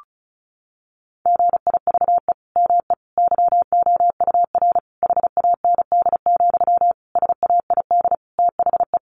Morse code, heard as a keyed tone, about 34 wpm, 700 Hz. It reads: GIVE ME YOUR HAND, SAID THE
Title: Sentences from Top 1500 Words 35wpm